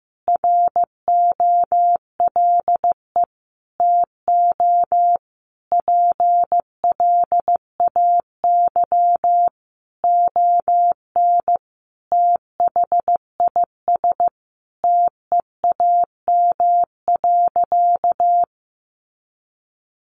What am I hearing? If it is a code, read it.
Morse code, heard as a keyed tone, about 15 wpm, 700 Hz. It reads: ROLE TO PLAY ON THIS TEAM.
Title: Sentences from Top 1500 Words 15wpm